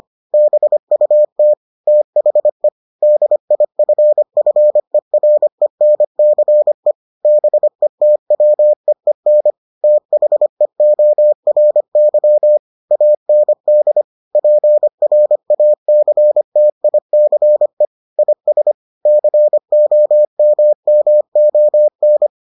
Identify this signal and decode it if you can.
Morse code, heard as a keyed tone, about 25 wpm, 600 Hz. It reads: BUT THE DIFFERENCE BETWEEN THEORY AND PRACTICE IS COMMON